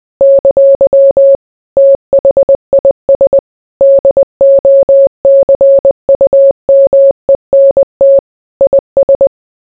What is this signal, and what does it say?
Morse code, heard as a keyed tone, about 20 wpm, 550 Hz. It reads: ! THIS DOCUMENT IS